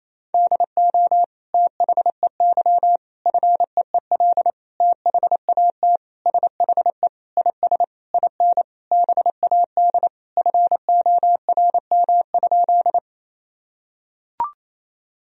Morse code, heard as a keyed tone, about 28 wpm, 700 Hz. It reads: DO THEY FEEL THAT SHE IS IN BAD FORM?
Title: Sentences from Top 1000 Words - Review Part 1 - 28wpm